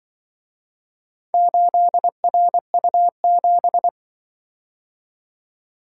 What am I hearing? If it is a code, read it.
Morse code, heard as a keyed tone, about 25 wpm, 700 Hz. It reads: 8RU7